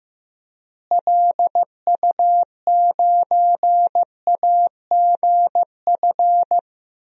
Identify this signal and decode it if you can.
Morse code, heard as a keyed tone, about 15 wpm, 700 Hz. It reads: LU9AGF
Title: DXCC-100 Call Signs - 6 Characters - Repeated 2x - 15wpm